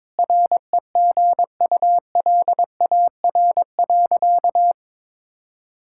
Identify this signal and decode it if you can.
Morse code, heard as a keyed tone, about 22 wpm, 700 Hz. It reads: REGULAR.
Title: Sentences from Top 1000 Words - Review Part 6 - 22wpm